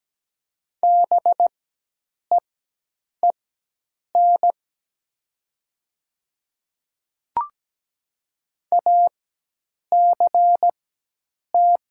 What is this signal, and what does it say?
Morse code, heard as a keyed tone, about 17 wpm, spaced as for 8 wpm, 700 Hz. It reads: BEEN ACT
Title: 4x Character Spacing - Top 300 Words - 17wpm